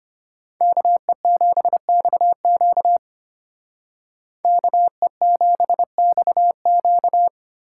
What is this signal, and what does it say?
Morse code, heard as a keyed tone, about 33 wpm, 700 Hz. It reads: KE7XQ KE7XQ